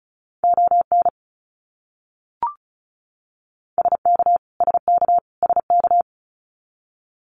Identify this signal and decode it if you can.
Morse code, heard as a keyed tone, about 34 wpm, 700 Hz. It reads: ON SK SK SK
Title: Canadian Provinces and Territories - Repeated 3x - 35wpm